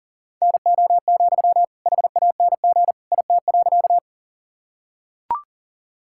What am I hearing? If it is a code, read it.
Morse code, heard as a keyed tone, about 40 wpm, 700 Hz. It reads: NO, HANG IT.
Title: Sentences from Top 1600 Words 40wpm